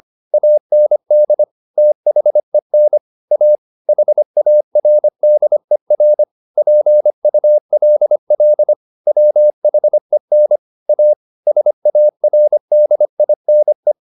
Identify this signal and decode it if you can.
Morse code, heard as a keyed tone, about 25 wpm, 600 Hz. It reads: AND THEN A HARDER PULL WHEN A SARDINE
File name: the old man and the sea part 1 25wpm